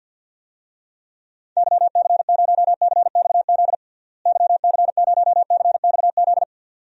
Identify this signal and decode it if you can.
Morse code, heard as a keyed tone, about 52 wpm, 700 Hz. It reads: YC0CXZ YC0CXZ
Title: DXCC-100 Call Signs - 6 Characters - Repeated 2x - 50wpm